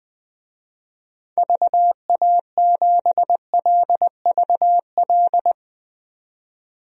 Morse code, heard as a keyed tone, about 21 wpm, 700 Hz. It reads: VA7LVL
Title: DXCC-100 Call Signs - 6 Characters - 20wpm